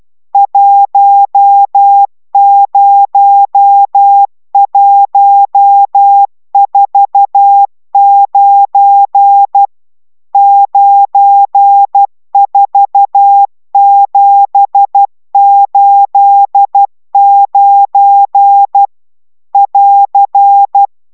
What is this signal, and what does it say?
Morse code, heard as a keyed tone, about 12 wpm, 800 Hz. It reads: 10149 94789 <AR>
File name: taviro tanulas / cw 75 31 10